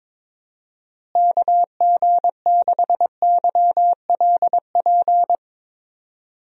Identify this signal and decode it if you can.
Morse code, heard as a keyed tone, about 21 wpm, 700 Hz. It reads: KG6YLP
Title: USA Call Signs - 6 Characters - Repeated 3x - 22wpm